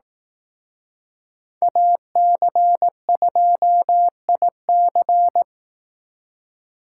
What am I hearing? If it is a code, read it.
Morse code, heard as a keyed tone, about 19 wpm, 700 Hz. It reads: AC2IC
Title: Speed Racing - USA 2x2 Call Signs 15wpm